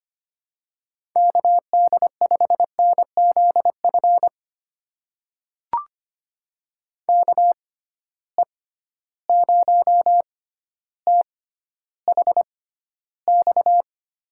Morse code, heard as a keyed tone, about 25 wpm, spaced as for 13 wpm, 700 Hz. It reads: KD5NZF KE0THX
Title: USA Call Signs - 6 Characters - 6x Character Spacing - 25wpm